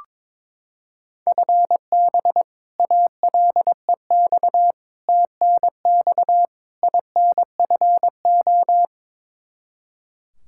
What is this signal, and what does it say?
Morse code, heard as a keyed tone, about 22 wpm, 700 Hz. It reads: FB ALEX TNX INFO
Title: Practice QSO - Exchange 3 - More Info - 22wpm